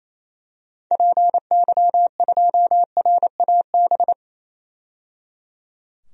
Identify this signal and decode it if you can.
Morse code, heard as a keyed tone, about 28 wpm, 700 Hz. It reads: PY2RAB
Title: DXCC-100 Call Signs - 6 Characters - 28wpm